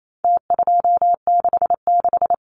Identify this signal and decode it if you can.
Morse code, heard as a keyed tone, about 28 wpm, 700 Hz. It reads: T266
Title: Numbers - Contest Style - 4-Digit 28wpm